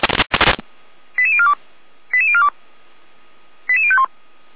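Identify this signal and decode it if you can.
Five-tone selective call (ZVEI), heard as quick runs of five tones, tones of about 70 ms. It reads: 88042 88042 88051
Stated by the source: ZVEI five-tone selective call 88103, received by SDR, from the Feuerwehr Orlamünde archive.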